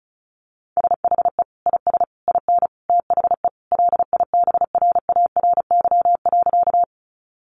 Morse code, heard as a keyed tone, about 34 wpm, 700 Hz. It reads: SHE IS IN THE LIBRARY.
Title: Sentences from Top 600 Words 35wpm